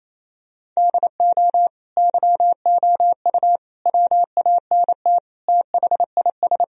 Morse code, heard as a keyed tone, about 28 wpm, 700 Hz. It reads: DO YOU WANT THIS
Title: Sentences from Top 1000 Words - Review Part 2 - 28wpm